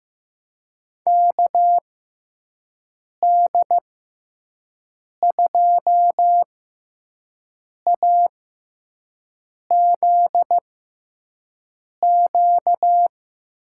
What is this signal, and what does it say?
Morse code, heard as a keyed tone, about 15 wpm, spaced as for 5 wpm, 700 Hz. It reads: KD2AZQ
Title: USA Call Signs - 6x Character Spacing - 15wpm